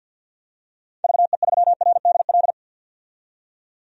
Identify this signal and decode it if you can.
Morse code, heard as a keyed tone, about 49 wpm, 700 Hz. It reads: VE3RDL
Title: DXCC-100 Call Signs - 6 Characters - 50wpm